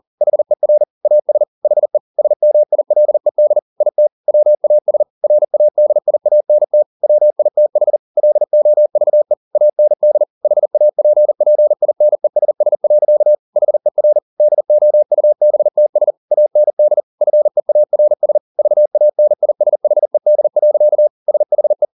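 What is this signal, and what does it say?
Morse code, heard as a keyed tone, about 40 wpm, 600 Hz. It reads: HER AS HE SMILED IT WAS RADIANT WITH LOVE AND HAPPINESS. HER DOUBTS AND FEARS VANISHED. SHE
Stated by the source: Morse code at 40 wpm, practice text, CW med Gnister.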